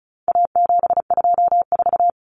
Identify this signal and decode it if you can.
Morse code, heard as a keyed tone, about 34 wpm, 700 Hz. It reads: A724
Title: Common Cut Numbers - 4-Digit 35wpm